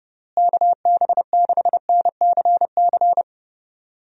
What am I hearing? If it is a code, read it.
Morse code, heard as a keyed tone, about 32 wpm, 700 Hz. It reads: KB6NCC
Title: USA Call Signs - 6 Characters - Repeated 2x - 30wpm